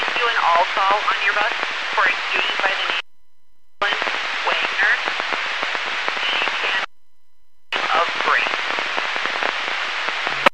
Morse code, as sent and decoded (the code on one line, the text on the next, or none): none